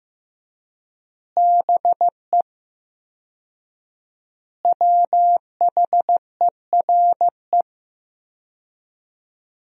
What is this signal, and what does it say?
Morse code, heard as a keyed tone, about 15 wpm, 700 Hz. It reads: BE WHERE
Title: 4x Word-Spacing - Sentences from Top 200 Words - 15wpm